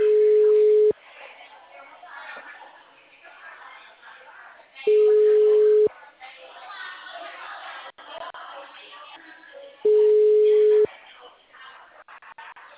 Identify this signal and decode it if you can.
Ringback tone (Europe, 425 Hz), with speech.